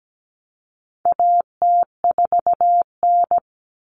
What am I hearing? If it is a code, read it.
Morse code, heard as a keyed tone, about 17 wpm, 700 Hz. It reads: AT4N
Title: Common Cut Numbers - 4-Digit 17wpm